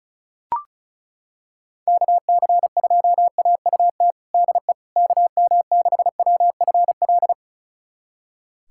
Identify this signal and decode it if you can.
Morse code, heard as a keyed tone, about 36 wpm, 700 Hz. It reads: KC2AUT DE KM6WFL
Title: USA Callsign DE Callsign - 6 Characters - 35wpm